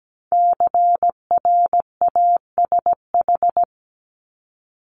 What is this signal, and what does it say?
Morse code, heard as a keyed tone, about 17 wpm, 700 Hz. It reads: CRASH
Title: Top 1751-2000 Words 17wpm